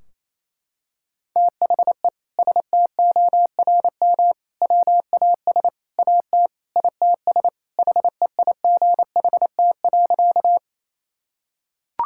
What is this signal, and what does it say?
Morse code, heard as a keyed tone, about 28 wpm, 700 Hz. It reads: THE STORM WAS AT ITS HEIGHT.
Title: Sentences from Top 1600 Words 28wpm